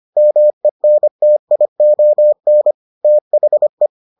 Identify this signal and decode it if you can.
Morse code, heard as a keyed tone, about 25 wpm, 600 Hz. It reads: MENTION THE